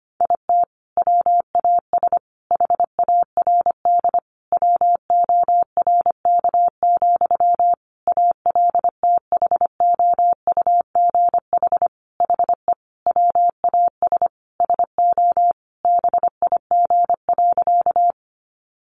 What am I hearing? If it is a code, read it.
Morse code, heard as a keyed tone, about 25 wpm, 700 Hz. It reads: IT WAS HARD WORK, ALTHOUGH HE WAS SO BIG.
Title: Sentences from Top 600 Words 25wpm